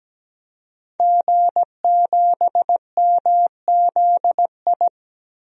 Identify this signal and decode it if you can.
Morse code, heard as a keyed tone, about 17 wpm, 700 Hz. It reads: G7MZI
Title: DXCC-100 Call Signs - 5 Characters - 17wpm